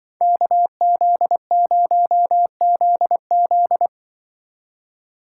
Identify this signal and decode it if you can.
Morse code, heard as a keyed tone, about 25 wpm, 700 Hz. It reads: KZ0ZZ
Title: Speed Racing - USA 2x2 Call Signs 20wpm